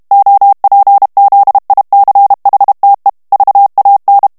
Morse code, heard as a keyed tone, about 32 wpm, 800 Hz. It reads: OPZICHTE VAN
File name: morse mp3 / HOLLANDAIS texte clair 1 vitesse 30 mots minute